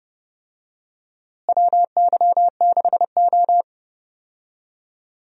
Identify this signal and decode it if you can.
Morse code, heard as a keyed tone, about 30 wpm, 700 Hz. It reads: WY6O
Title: USA Call Signs - 4 Characters - Repeated 2x - 30wpm